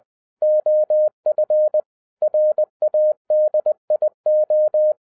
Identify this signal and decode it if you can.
Morse code, heard as a keyed tone, about 20 wpm, 610 Hz. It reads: OF RADIO